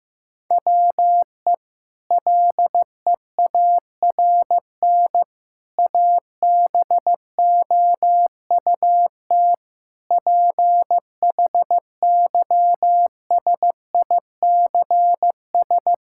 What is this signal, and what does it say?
Morse code, heard as a keyed tone, about 15 wpm, 700 Hz. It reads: WE LEARN ABOUT PHYSICS
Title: Sentences from Top 1000 Words - Review Part 5 - 15wpm